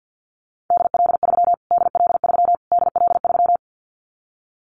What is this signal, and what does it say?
Morse code, heard as a keyed tone, about 51 wpm, 700 Hz. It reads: DL3 DL3 DL3
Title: Callsign Prefixes for Top 25 DXCC - Repeated 3x - 50wpm